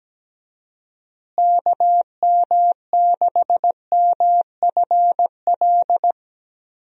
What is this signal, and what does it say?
Morse code, heard as a keyed tone, about 17 wpm, 700 Hz. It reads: KM6MFL
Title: USA Call Signs - 6 Characters - Repeated 2x - 17wpm